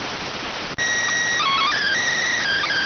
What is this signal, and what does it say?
SSTV (slow-scan television), PD 290.